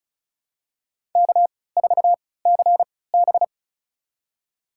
Morse code, heard as a keyed tone, about 36 wpm, spaced as for 20 wpm, 700 Hz. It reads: K4CB
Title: USA Call Signs - 4 Characters - 3x Character Spacing - 35wpm